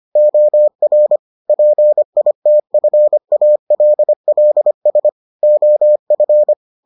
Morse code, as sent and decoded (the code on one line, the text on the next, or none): --- .-. / .--. .. - ..-. .- .-.. .-.. ... / --- ..-.
OR PITFALLS OF